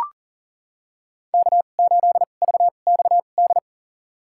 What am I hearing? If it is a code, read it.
Morse code, heard as a keyed tone, about 40 wpm, spaced as for 29 wpm, 700 Hz. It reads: K8VXD
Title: USA Call Signs - 5 Characters - 2x Character Spacing - 40wpm